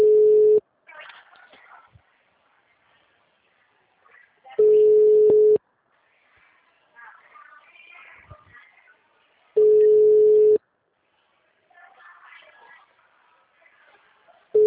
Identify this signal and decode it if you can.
Ringback tone (Europe, 425 Hz), with speech.
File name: out- -2004- - - .73977